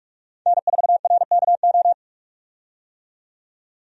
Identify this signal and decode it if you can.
Morse code, heard as a keyed tone, about 42 wpm, 700 Hz. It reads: N4RKO
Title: USA Call Signs - 5 Characters - 45wpm